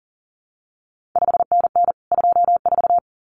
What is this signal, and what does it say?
Morse code, heard as a keyed tone, about 41 wpm, 700 Hz. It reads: 5NN 24
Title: CQ WW DX Contest Exchange Encore - 40wpm